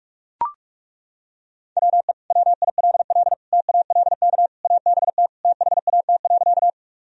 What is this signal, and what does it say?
Morse code, heard as a keyed tone, about 41 wpm, 700 Hz. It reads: WE WILL TALK ABT THAT.
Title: Sentences from Top 400 Words and Abbreviations 45wpm